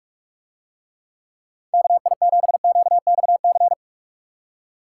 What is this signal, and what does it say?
Morse code, heard as a keyed tone, about 44 wpm, 700 Hz. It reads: KI7QXC